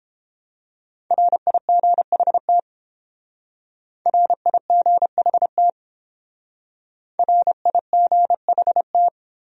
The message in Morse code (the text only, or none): RIGHT RIGHT RIGHT